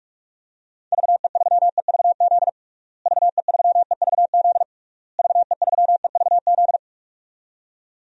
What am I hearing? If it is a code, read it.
Morse code, heard as a keyed tone, about 42 wpm, 700 Hz. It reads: VE3EVZ VE3EVZ VE3EVZ
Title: DXCC-100 Call Signs - 6 Characters - Repeated 3x - 45wpm